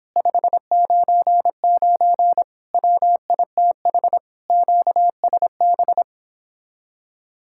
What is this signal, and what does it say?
Morse code, heard as a keyed tone, about 29 wpm, 700 Hz. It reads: 599 WITH QSB